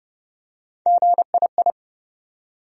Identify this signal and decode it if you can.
Morse code, heard as a keyed tone, about 31 wpm, 700 Hz. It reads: GII
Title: USA Call Sign Postfixes - 3 Letters 30wpm